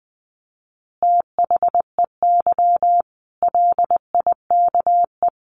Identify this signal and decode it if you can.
Morse code, heard as a keyed tone, about 19 wpm, 700 Hz. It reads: THEY LIKE
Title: Easy Phrases 20wpm